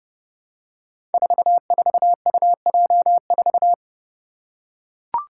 Morse code, heard as a keyed tone, about 30 wpm, 700 Hz. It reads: 44UJ4
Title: Five Challenging Letter-Number Combinations 30wpm